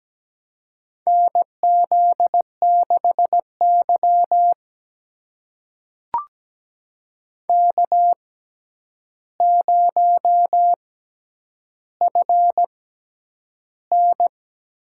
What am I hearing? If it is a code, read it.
Morse code, heard as a keyed tone, about 17 wpm, spaced as for 9 wpm, 700 Hz. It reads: NZ6Y K0FN
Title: USA Call Signs - 6x Character Spacing - 17wpm